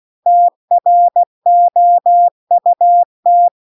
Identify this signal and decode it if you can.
Morse code, heard as a keyed tone, about 16 wpm, 700 Hz. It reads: TROUT
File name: CW-The-Big-Twoheaded-River-2-16WPM